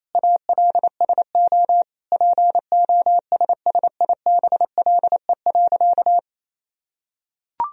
Morse code, heard as a keyed tone, about 28 wpm, 700 Hz. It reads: ALSO POSSIBLE.
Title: Sentences from Top 1000 Words - Review Part 1 - 28wpm